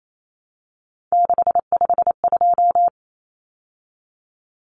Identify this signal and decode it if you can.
Morse code, heard as a keyed tone, about 28 wpm, 700 Hz. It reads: 652
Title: Numbers - 3-Digit 28wpm